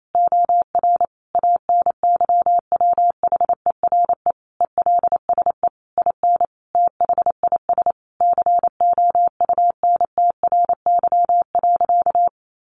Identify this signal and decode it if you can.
Morse code, heard as a keyed tone, about 28 wpm, 700 Hz. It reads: OR ANYWHERE ELSE IN THIS COUNTRY.
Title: Sentences from Top 1400 Words 28wpm